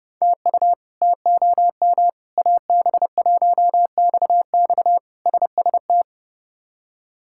Morse code, heard as a keyed tone, about 32 wpm, 700 Hz. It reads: TU TOM AB1XX SST
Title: K1USN SST QSO Practice - 30wpm